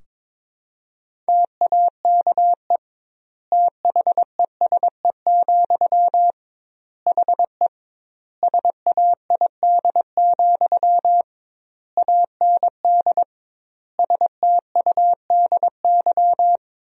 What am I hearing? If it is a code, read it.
Morse code, heard as a keyed tone, about 22 wpm, 700 Hz. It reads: TAKE THESE, HE SAID, AND STUDY
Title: 2x Word-Spacing - Sentences from Top 300 Words - 22wpm